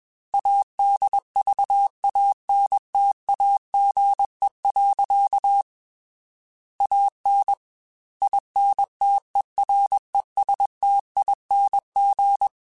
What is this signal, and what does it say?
Morse code, heard as a keyed tone, about 21 wpm, 800 Hz. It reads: ADVANTAGE. AN INTERESTING
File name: Technical Subjects / Technical Subjects-mp3s / Sw Pwr Sup @20